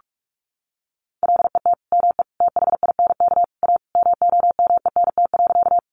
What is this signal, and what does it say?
Morse code, heard as a keyed tone, about 44 wpm, 700 Hz. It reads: LET ME THINK A MOMENT.
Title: Sentences from Top 1000 Words 45wpm